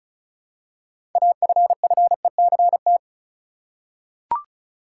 Morse code, heard as a keyed tone, about 34 wpm, 700 Hz. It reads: AFFECT
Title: Six Letter Words - 35wpm